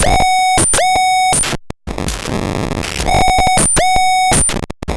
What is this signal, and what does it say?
There is no code.